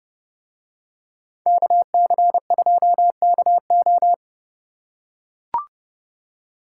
Morse code, heard as a keyed tone, about 28 wpm, 700 Hz. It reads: KC2KO